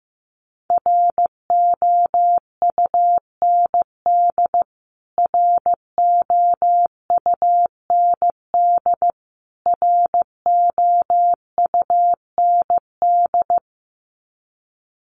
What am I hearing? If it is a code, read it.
Morse code, heard as a keyed tone, about 15 wpm, 700 Hz. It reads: ROUND ROUND ROUND
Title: Top 200 Words Repeated 15wpm